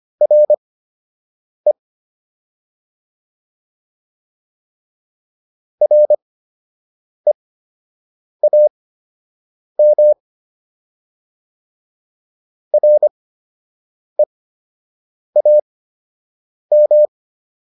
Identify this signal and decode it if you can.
Morse code, heard as a keyed tone, about 25 wpm, spaced as for 7 wpm, 600 Hz. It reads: RE REAM REAM